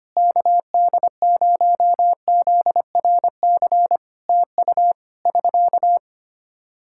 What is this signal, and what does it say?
Morse code, heard as a keyed tone, about 25 wpm, 700 Hz. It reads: KD0ZRC TU <SK>
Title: K1USN SST QSO Practice 25wpm